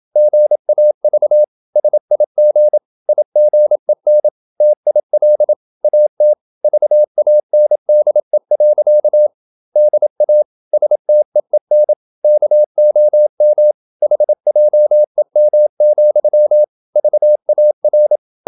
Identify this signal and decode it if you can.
Morse code, heard as a keyed tone, about 27 wpm, 600 Hz. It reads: GAV SIG IGEN TIL AT VANDE. DA STEEN KOM HJEM, VAR
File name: Ferie QRQ 27wpm